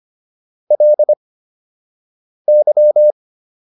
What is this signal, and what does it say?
Morse code, heard as a keyed tone, about 25 wpm, spaced as for 6 wpm, 600 Hz. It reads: LY